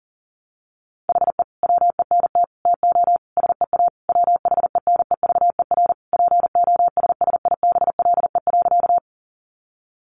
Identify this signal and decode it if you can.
Morse code, heard as a keyed tone, about 40 wpm, 700 Hz. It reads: HE WENT TO SEA WHENEVER POSSIBLE.
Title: Sentences from Top 1200 Words 40wpm